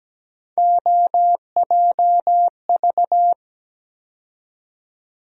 Morse code, heard as a keyed tone, about 17 wpm, 700 Hz. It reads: OJV
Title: USA Call Sign Postfixes - 3 Letters 17wpm